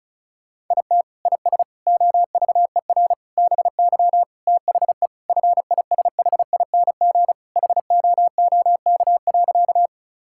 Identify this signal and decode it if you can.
Morse code, heard as a keyed tone, about 34 wpm, 700 Hz. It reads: IT IS OVER BY THE FISHING HOOK.
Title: Sentences from Top 1600 Words 35wpm